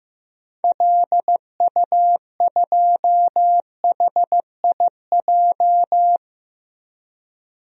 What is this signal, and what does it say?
Morse code, heard as a keyed tone, about 15 wpm, 700 Hz. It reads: LU2HIJ
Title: DXCC-100 Call Signs - 6 Characters - 15wpm